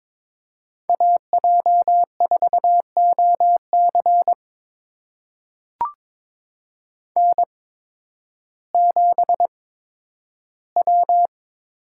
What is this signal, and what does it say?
Morse code, heard as a keyed tone, about 22 wpm, spaced as for 12 wpm, 700 Hz. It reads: AJ4OC N7W
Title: USA Call Signs - 8x Character Spacing - 22wpm